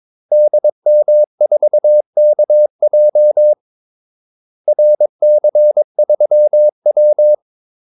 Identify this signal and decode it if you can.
Morse code, heard as a keyed tone, about 22 wpm, 600 Hz. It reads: DM4KJ RC3W
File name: Callsigns-22wpm